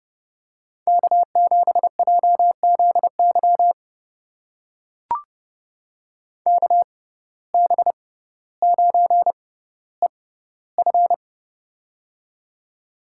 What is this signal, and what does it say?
Morse code, heard as a keyed tone, about 30 wpm, spaced as for 15 wpm, 700 Hz. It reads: K7JZY KB9EF
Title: USA Call Signs - 5 Characters - 6x Character Spacing - 30wpm